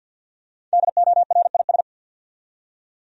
Morse code, heard as a keyed tone, about 51 wpm, 700 Hz. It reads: DORIS
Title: USA Common Names - Max 5 Letters - 50wpm